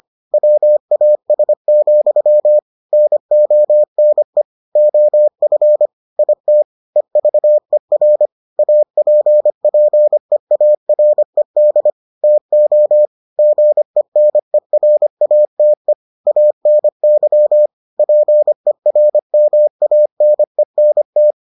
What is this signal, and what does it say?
Morse code, heard as a keyed tone, about 25 wpm, 600 Hz. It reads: WAS, NONE OF IT EVER APPEARED TO GENERATE ANY PERMANENT